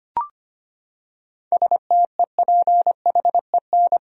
Morse code, heard as a keyed tone, about 25 wpm, 700 Hz. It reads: STEPHEN
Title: USA Common Names - 25wpm